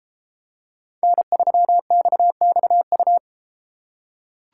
Morse code, heard as a keyed tone, about 32 wpm, 700 Hz. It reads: N3XXU